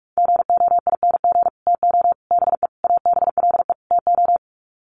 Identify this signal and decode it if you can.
Morse code, heard as a keyed tone, about 44 wpm, 700 Hz. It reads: GOING TO BE ABLE TO
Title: Sets of 5 Words 45wpm